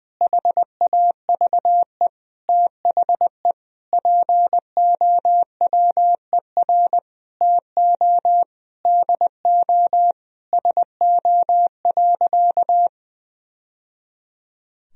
Morse code, heard as a keyed tone, about 20 wpm, 700 Hz. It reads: HAVE THE POWER TO DO SO.
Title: Sentences from Top 1000 Words - Review Part 4 - 20wpm